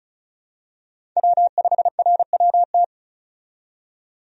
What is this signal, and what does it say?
Morse code, heard as a keyed tone, about 36 wpm, 700 Hz. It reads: W5RWT
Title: USA Call Signs - 5 Characters - 35wpm